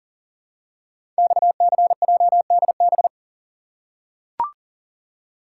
Morse code, heard as a keyed tone, about 40 wpm, 700 Hz. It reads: XCJDB
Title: Code Groups - 5 Letters - 40wpm